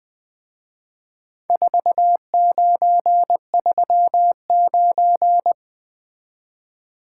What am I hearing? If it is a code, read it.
Morse code, heard as a keyed tone, about 22 wpm, 700 Hz. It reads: 4939
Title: Speed-Racing - Four Letter-Number Combinations - 20wpm